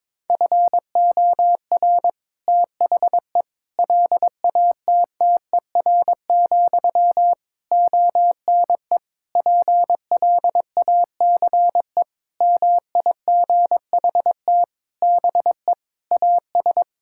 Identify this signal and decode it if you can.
Morse code, heard as a keyed tone, about 22 wpm, 700 Hz. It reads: FOR THE LATTER, ONE PLACE MIGHT BE AS
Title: Sentences from Top 1600 Words 22wpm